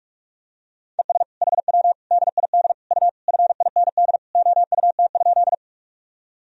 Morse code, heard as a keyed tone, about 46 wpm, 700 Hz. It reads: ES HW DID U FIND OUT?
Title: Sentences from Top 100 Words and Abbreviations 45wpm